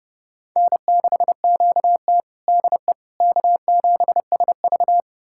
Morse code, heard as a keyed tone, about 30 wpm, 700 Hz. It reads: N6QT DE K7SV
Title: USA Callsign DE Callsign - 4 Characters - 30wpm